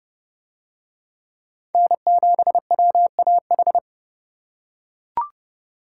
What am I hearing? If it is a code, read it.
Morse code, heard as a keyed tone, about 27 wpm, 700 Hz. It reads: N7WAH